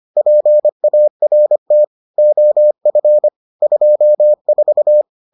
Morse code, heard as a keyed tone, about 25 wpm, 600 Hz. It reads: PART OF 24